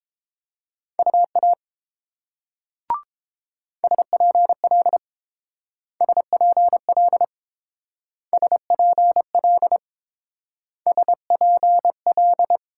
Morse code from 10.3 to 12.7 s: SPL

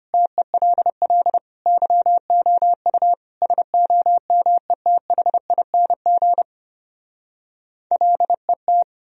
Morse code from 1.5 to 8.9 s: YOU SOMETHING LET

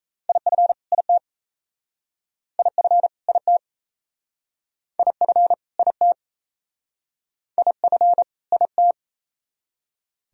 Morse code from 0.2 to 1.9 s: IF IT

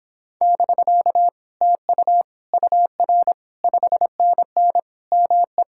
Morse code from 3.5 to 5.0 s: 5NN

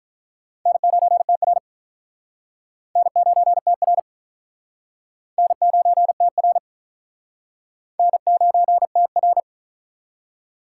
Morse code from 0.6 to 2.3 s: N9TR

